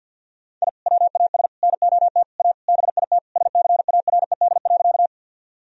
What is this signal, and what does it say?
Morse code, heard as a keyed tone, about 50 wpm, 700 Hz. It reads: I WAS NOT A BIT SCARED.
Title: Sentences from Top 1000 Words - Review Part 4 - 50wpm